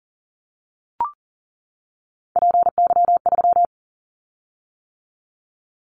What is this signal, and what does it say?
Morse code, heard as a keyed tone, about 40 wpm, 700 Hz. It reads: PY3
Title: Callsign Prefixes for Top 50 DXCC - 40wpm